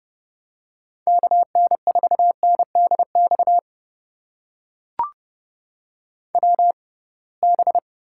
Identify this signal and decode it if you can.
Morse code, heard as a keyed tone, about 29 wpm, spaced as for 22 wpm, 700 Hz. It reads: KN4NDX WB
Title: USA Call Signs - 6 Characters - 6x Character Spacing - 30wpm